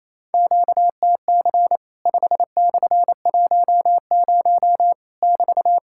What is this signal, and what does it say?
Morse code, heard as a keyed tone, about 28 wpm, 700 Hz. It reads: QTC 5/10 <BT>
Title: Worked All Europe DX Contest - QTC Exchange - 28wpm